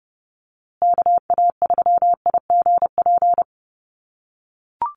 Morse code, heard as a keyed tone, about 30 wpm, 700 Hz. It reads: KA3IGP
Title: USA Call Signs Repeated 2x 30wpm